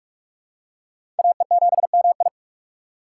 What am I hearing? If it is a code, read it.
Morse code, heard as a keyed tone, about 45 wpm, 700 Hz. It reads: AE7MI
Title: USA Call Signs - 5 Characters - 2x Character Spacing - 45wpm